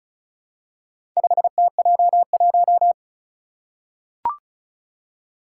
Morse code, heard as a keyed tone, about 35 wpm, 690 Hz. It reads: 5TJ1
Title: Four Challenging Letter-Number Combinations 35wpm